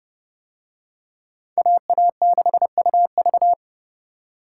Morse code, heard as a keyed tone, about 29 wpm, 700 Hz. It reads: AA6UV